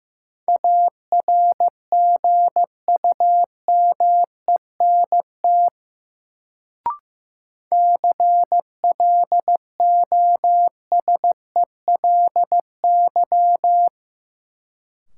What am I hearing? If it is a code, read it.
Morse code, heard as a keyed tone, about 15 wpm, 700 Hz. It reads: ARGUMENT CLOSELY
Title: Sentences from Top 1400 Words - New Word Drill Alt - 15wpm